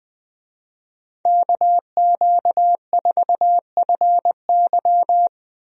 Morse code, heard as a keyed tone, about 20 wpm, 700 Hz. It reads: KQ4FY
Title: USA Call Signs - 5 Characters - Repeated 2x - 20wpm